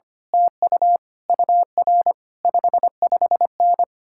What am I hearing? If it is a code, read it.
Morse code, heard as a keyed tone, about 24 wpm, 700 Hz. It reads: TU UR 55N
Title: Speed-Racing Common QSO Phrases Encore - 25wpm